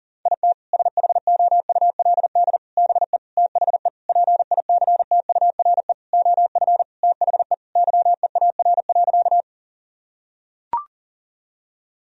Morse code, heard as a keyed tone, about 40 wpm, 700 Hz. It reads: IT SHOULD BE THE PICTURE OF THE YEAR.
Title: Sentences from Top 1000 Words - Review Part 1 - 40wpm